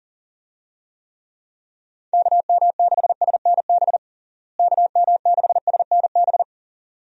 Morse code, heard as a keyed tone, about 37 wpm, 700 Hz. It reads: KM6SNB KM6SNB
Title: USA Call Signs - 6 Characters - Repeated 2x - 40wpm